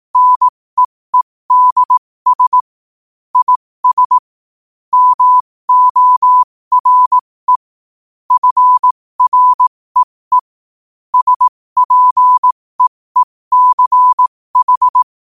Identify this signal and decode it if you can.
Morse code, heard as a keyed tone, about 18 wpm, spaced as for 15 wpm, 1000 Hz. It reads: NEEDS IS MORE FREE SPEECH
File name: qotdfiles / 15wpm